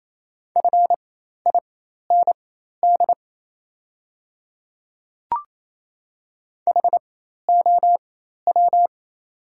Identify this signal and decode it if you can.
Morse code, heard as a keyed tone, about 28 wpm, spaced as for 13 wpm, 690 Hz. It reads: FIND HOW
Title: 4x Character Spacing - Top 100 Words - 28wpm